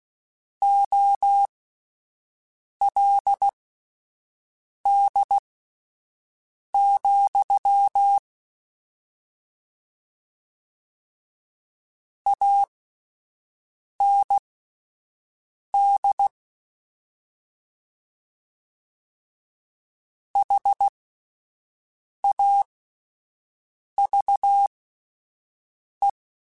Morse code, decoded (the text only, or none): OLD, AND HAVE